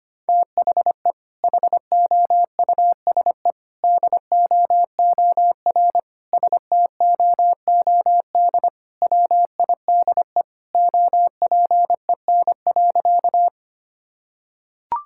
Morse code, heard as a keyed tone, about 25 wpm, 700 Hz. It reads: THE HOUSE DOOR STOOD WIDE OPEN.
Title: Sentences from Top 1000 Words - Review Part 6 - 25wpm